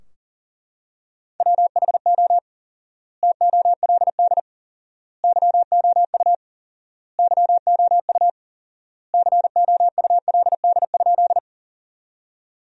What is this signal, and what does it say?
Morse code, heard as a keyed tone, about 37 wpm, 700 Hz. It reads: WHO TOLD YOU YOU COULD?
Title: 4x Word-Spacing - Sentences from Top 400 Words - 40wpm